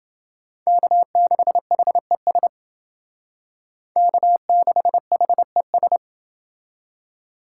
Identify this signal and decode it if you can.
Morse code, heard as a keyed tone, about 28 wpm, 700 Hz. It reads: K6HES K6HES